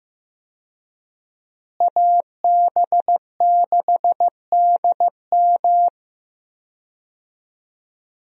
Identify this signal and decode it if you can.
Morse code, heard as a keyed tone, about 15 wpm, 700 Hz. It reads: AB6DM
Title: USA Call Signs - 5 Characters - Repeated 2x - 15wpm